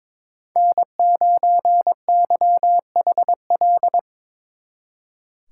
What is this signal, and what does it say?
Morse code, heard as a keyed tone, about 25 wpm, 700 Hz. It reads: N9YHL
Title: Speed Racing - USA Call Signs 22wpm